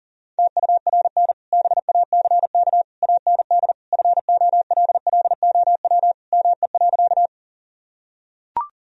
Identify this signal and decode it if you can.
Morse code, heard as a keyed tone, about 40 wpm, 700 Hz. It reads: TURN BACK AND FOLLOW ME.
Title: Sentences from Top 1000 Words - Review Part 1 - 40wpm